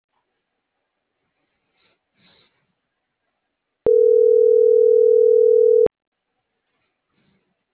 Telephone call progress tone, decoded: ringback tone

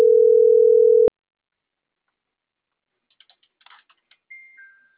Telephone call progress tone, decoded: ringback tone

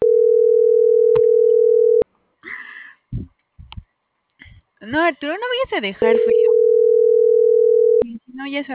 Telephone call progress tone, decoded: ringback tone

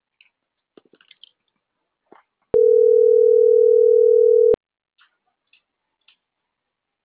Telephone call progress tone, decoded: ringback tone

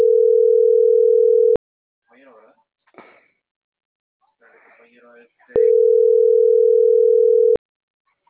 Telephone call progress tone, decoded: ringback tone

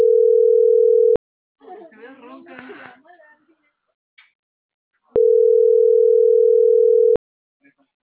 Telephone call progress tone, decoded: ringback tone

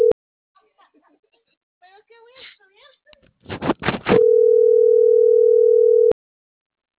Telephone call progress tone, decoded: ringback tone

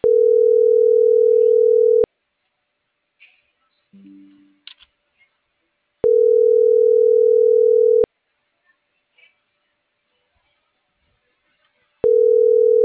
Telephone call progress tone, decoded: ringback tone

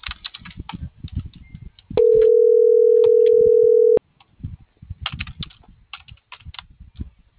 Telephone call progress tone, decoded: ringback tone

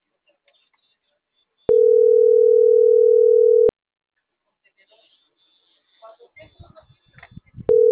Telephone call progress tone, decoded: ringback tone

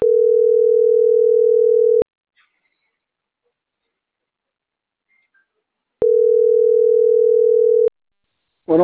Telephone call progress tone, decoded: ringback tone

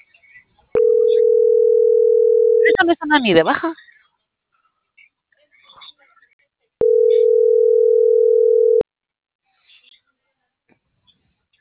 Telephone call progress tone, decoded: ringback tone

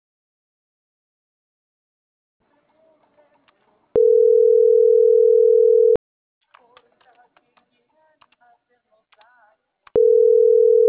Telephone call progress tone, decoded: ringback tone

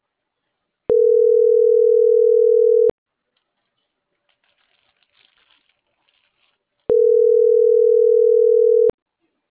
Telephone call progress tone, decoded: ringback tone

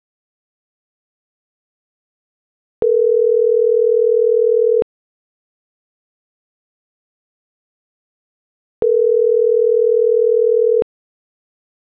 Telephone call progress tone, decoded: ringback tone

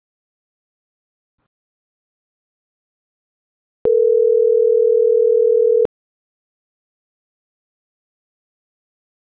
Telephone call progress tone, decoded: ringback tone

